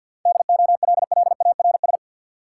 Morse code, heard as a keyed tone, about 49 wpm, 700 Hz. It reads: DOLLARS
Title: Sets of 5 Words - New Word Drill - 50wpm